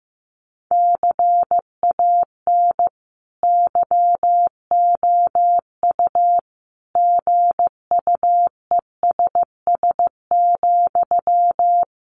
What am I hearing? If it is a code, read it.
Morse code, heard as a keyed tone, about 14 wpm, 700 Hz. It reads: CAN YOU GUESS,
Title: Sentences from Top 1500 Words 15wpm